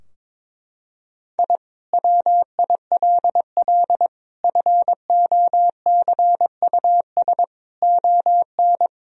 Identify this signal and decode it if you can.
Morse code, heard as a keyed tone, about 22 wpm, 700 Hz. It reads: I WILL FOCUS ON
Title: Sentences from Top 1000 Words - Review Part 4 - 22wpm